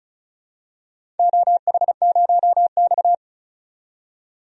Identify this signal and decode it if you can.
Morse code, heard as a keyed tone, about 35 wpm, 700 Hz. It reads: OH0X
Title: DXCC-100 Call Signs - 4 Characters - 35wpm